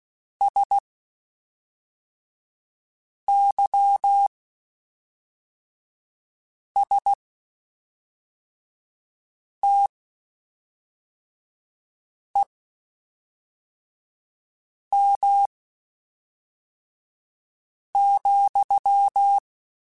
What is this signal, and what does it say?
Morse code, heard as a keyed tone, about 16 wpm, spaced as for 3 wpm, 800 Hz. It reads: SYSTEM,
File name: Quotes Spaced / Quotes Spaced-mp3s / Airtight sp @5